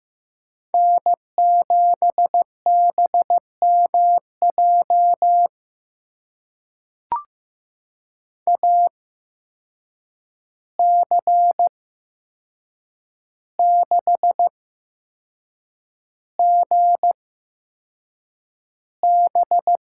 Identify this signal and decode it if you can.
Morse code, heard as a keyed tone, about 15 wpm, spaced as for 6 wpm, 700 Hz. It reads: N7BMJ AC6GB
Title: USA Call Signs - 5 Characters - 8x Character Spacing - 15wpm